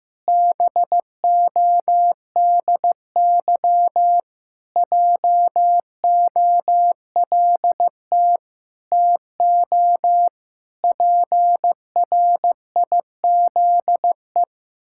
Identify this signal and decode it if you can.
Morse code, heard as a keyed tone, about 15 wpm, 700 Hz. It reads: BODY JOLT TO PRIZE
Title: Pangram Challenge v2 - 15wpm